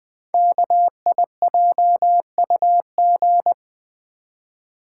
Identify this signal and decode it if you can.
Morse code, heard as a keyed tone, about 20 wpm, 700 Hz. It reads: KIJUG
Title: Code Groups - 5 Letters - 20wpm